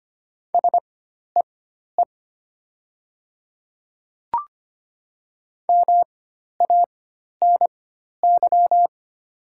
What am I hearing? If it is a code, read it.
Morse code, heard as a keyed tone, about 25 wpm, spaced as for 12 wpm, 700 Hz. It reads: SEE MANY
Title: 4x Character Spacing - Top 100 Words - 25wpm